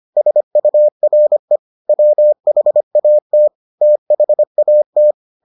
Morse code, heard as a keyed tone, about 25 wpm, 600 Hz. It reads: SURE WHAT THAT